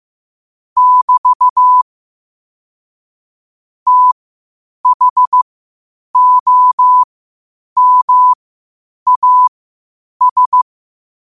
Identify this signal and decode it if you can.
Morse code, heard as a keyed tone, about 15 wpm, spaced as for 8 wpm, 1000 Hz. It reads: <BT> THOMAS